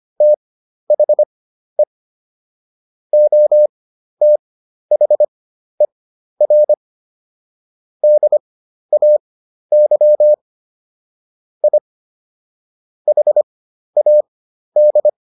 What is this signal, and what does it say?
Morse code, heard as a keyed tone, about 25 wpm, spaced as for 12 wpm, 600 Hz. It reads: THE OTHER DAY I HAD